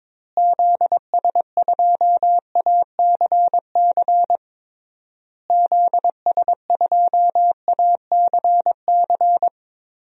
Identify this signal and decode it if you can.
Morse code, heard as a keyed tone, about 22 wpm, 700 Hz. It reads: ZS2ACC ZS2ACC
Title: DXCC-100 Call Signs - 6 Characters - Repeated 3x - 22wpm